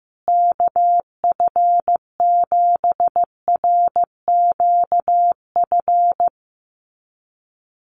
Morse code, heard as a keyed tone, about 15 wpm, 700 Hz. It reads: KF7RQF
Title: USA Call Signs Repeated 2x 15wpm